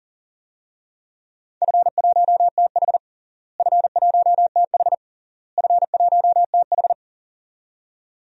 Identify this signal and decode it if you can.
Morse code, heard as a keyed tone, about 40 wpm, 700 Hz. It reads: F1TH F1TH F1TH
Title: DXCC-100 Call Signs - 4 Characters - Repeated 3x - 40wpm